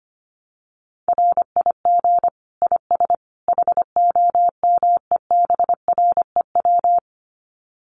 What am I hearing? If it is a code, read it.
Morse code, heard as a keyed tone, about 25 wpm, 700 Hz. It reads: RIG IS HOMEBREW
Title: Common QSO Phrases 25wpm